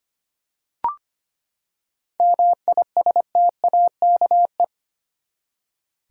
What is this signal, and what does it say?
Morse code, heard as a keyed tone, about 25 wpm, 700 Hz. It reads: MISTAKE
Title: Seven Letter Words - 25wpm